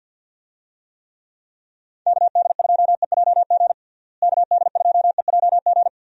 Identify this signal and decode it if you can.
Morse code, heard as a keyed tone, about 51 wpm, 700 Hz. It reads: KD2EJG KD2EJG
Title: USA Call Signs - 6 Characters - Repeated 3x - 50wpm